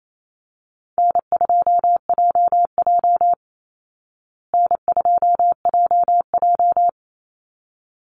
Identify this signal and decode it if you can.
Morse code, heard as a keyed tone, about 28 wpm, 700 Hz. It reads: N2JJ N2JJ
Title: Easy Call Signs Repeated 2x - 28wpm